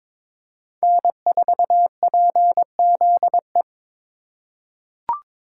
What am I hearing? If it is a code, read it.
Morse code, heard as a keyed tone, about 22 wpm, 700 Hz. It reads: N4PZE
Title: USA Call Signs - 5 Characters - 8x Character Spacing - 22wpm